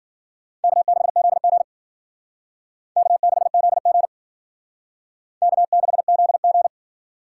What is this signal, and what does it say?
Morse code, heard as a keyed tone, about 53 wpm, 700 Hz. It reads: K6ZG K6ZG K6ZG